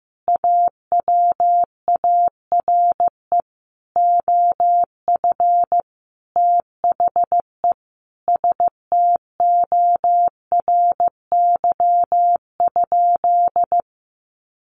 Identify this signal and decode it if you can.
Morse code, heard as a keyed tone, about 15 wpm, 700 Hz. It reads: AWARE OF THE STORY?
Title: Sentences from Top 1200 Words 15wpm